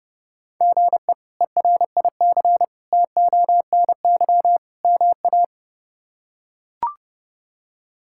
Morse code, heard as a keyed tone, about 30 wpm, 700 Hz. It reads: GE ERIC TONY MA
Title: K1USN SST QSO Practice - 30wpm